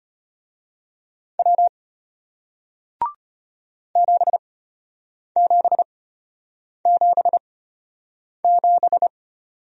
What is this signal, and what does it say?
Morse code, heard as a keyed tone, about 33 wpm, spaced as for 8 wpm, 700 Hz. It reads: W 7777